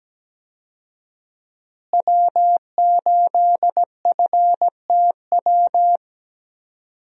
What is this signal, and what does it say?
Morse code, heard as a keyed tone, about 17 wpm, 700 Hz. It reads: W8FTW
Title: USA Call Signs - 5 Characters - Repeated 2x - 17wpm